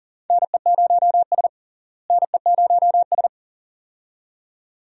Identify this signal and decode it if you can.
Morse code, heard as a keyed tone, about 37 wpm, 700 Hz. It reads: NE0S NE0S
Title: USA Call Signs - 4 Characters - Repeated 2x - 40wpm